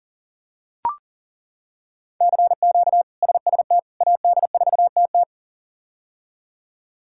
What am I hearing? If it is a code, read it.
Morse code, heard as a keyed tone, about 40 wpm, 700 Hz. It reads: CQ SST AD4TT
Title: K1USN SST QSO Practice - 40wpm